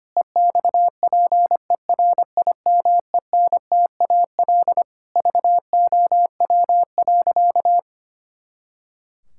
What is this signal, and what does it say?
Morse code, heard as a keyed tone, about 25 wpm, 700 Hz. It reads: EXPERIMENTAL VOW.
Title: Pangram Challenge v2 - 25wpm